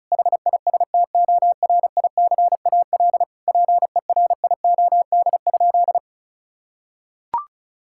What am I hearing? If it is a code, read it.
Morse code, heard as a keyed tone, about 36 wpm, 700 Hz. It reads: HISTORICAL PERIOD?
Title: Sentences from Top 1000 Words - Review Part 5 - 35wpm